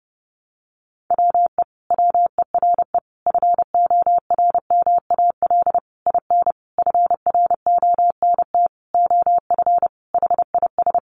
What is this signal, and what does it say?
Morse code, heard as a keyed tone, about 28 wpm, 700 Hz. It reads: WE WERE FORMAL IN FRONT OF HIS
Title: Sentences from Top 1300 Words 30wpm